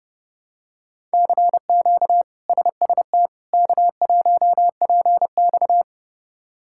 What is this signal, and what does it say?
Morse code, heard as a keyed tone, about 31 wpm, 700 Hz. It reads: CQ SST K1PX
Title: K1USN SST QSO Practice - 30wpm